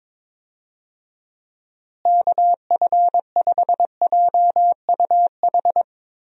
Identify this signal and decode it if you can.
Morse code, heard as a keyed tone, about 22 wpm, 700 Hz. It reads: KF5JUH
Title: USA Call Signs - 6 Characters - Repeated 2x - 22wpm